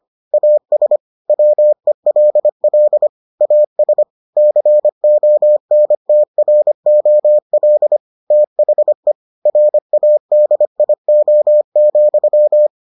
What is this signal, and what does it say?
Morse code, heard as a keyed tone, about 25 wpm, 600 Hz. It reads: AS WELL AS CONTROL THE RADIO,